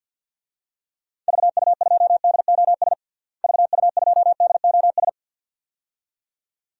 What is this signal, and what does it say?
Morse code, heard as a keyed tone, about 47 wpm, 700 Hz. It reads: VU2DOS VU2DOS
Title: DXCC-100 Call Signs - 6 Characters - Repeated 2x - 50wpm